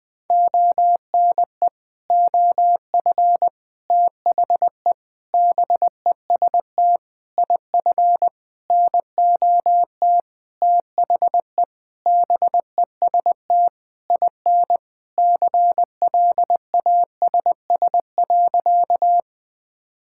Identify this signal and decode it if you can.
Morse code, heard as a keyed tone, about 20 wpm, 700 Hz. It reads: ONE OF THE BEST IF NOT THE BEST IN CLASS.
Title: Sentences with 8-Word Phrases - 20wpm